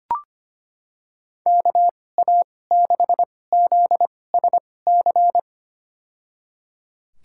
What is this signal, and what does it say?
Morse code, heard as a keyed tone, about 25 wpm, spaced as for 18 wpm, 700 Hz. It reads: KA6ZSC
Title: USA Call Signs - 6 Characters - 2x Character Spacing - 25wpm